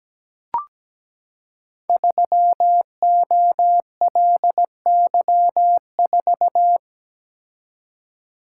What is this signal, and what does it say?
Morse code, heard as a keyed tone, about 17 wpm, 690 Hz. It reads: 3OLY4
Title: Code Groups - 5 Letter:Numbers - 17wpm